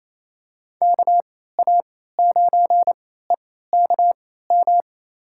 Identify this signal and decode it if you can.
Morse code, heard as a keyed tone, about 27 wpm, spaced as for 16 wpm, 700 Hz. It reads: KA9EKM
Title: USA Call Signs - 3x Character Spacing - 28wpm